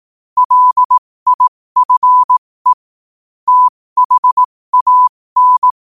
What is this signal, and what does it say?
Morse code, heard as a keyed tone, about 18 wpm, spaced as for 15 wpm, 1000 Hz. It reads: LIFE THAN
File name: qotdfiles / 15wpm